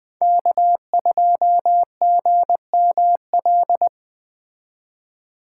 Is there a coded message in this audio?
Yes: Morse code from 0.2 to 3.9 s, heard as a keyed tone, about 20 wpm, 700 Hz. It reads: K2GML